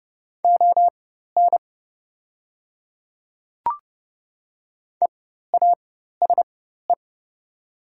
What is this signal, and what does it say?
Morse code, heard as a keyed tone, about 30 wpm, spaced as for 14 wpm, 700 Hz. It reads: ON EASE
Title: 4x Character Spacing - Top 300 Words - 30wpm